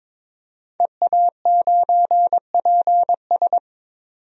Morse code, heard as a keyed tone, about 21 wpm, 700 Hz. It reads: EA9PS
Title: DXCC-100 Call Signs - 5 Characters - 22wpm